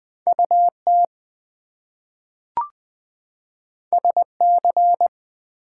Morse code, heard as a keyed tone, about 20 wpm, 700 Hz. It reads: UT SC
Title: US States and Canadian Provinces 20wpm